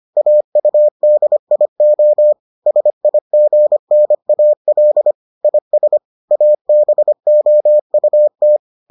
Morse code, heard as a keyed tone, about 25 wpm, 600 Hz. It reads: AUDIO SIGNAL IS ABOUT